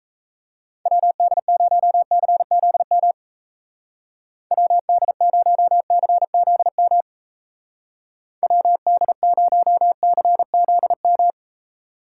Morse code, heard as a keyed tone, about 35 wpm, 700 Hz. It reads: WD0CZM WD0CZM WD0CZM